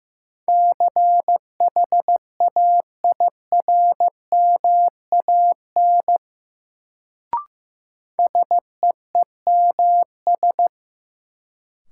Morse code, heard as a keyed tone, about 15 wpm, 700 Hz. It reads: CHAIRMAN SEEMS
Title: Sets of 3 Words - New Word Drill - 15wpm